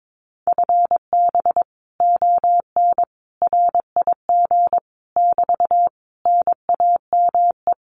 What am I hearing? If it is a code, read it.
Morse code, heard as a keyed tone, about 22 wpm, 700 Hz. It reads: FB ON RIG <BT> NAME
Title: Common QSO Phrases Encore 22wpm